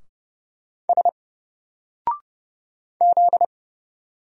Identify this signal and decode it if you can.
Morse code, heard as a keyed tone, about 29 wpm, 700 Hz. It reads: S Z
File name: Single-Letter-Number-30wpm